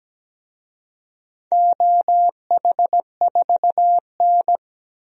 Morse code, heard as a keyed tone, about 17 wpm, 700 Hz. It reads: OH4N
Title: DXCC-100 Call Signs - 4 Characters - 17wpm